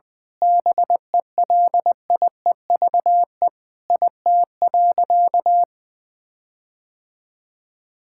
Morse code, heard as a keyed tone, about 20 wpm, 700 Hz. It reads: BELIEVE IT.
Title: Sentences with 4-Word Phrases - 20wpm